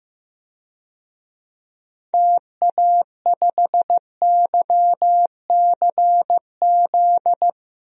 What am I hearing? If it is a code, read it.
Morse code, heard as a keyed tone, about 15 wpm, 700 Hz. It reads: TA5YCZ
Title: DXCC-100 Call Signs - 6 Characters - Repeated 3x - 15wpm